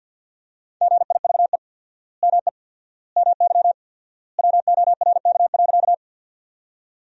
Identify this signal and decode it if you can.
Morse code, heard as a keyed tone, about 52 wpm, 700 Hz. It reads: GIVE ME MY WORK.
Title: 4x Word-Spacing - Sentences from Top 200 Words - 50wpm